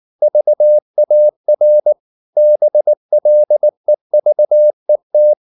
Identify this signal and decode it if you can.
Morse code, heard as a keyed tone, about 19 wpm, 600 Hz. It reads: VAR BLEVET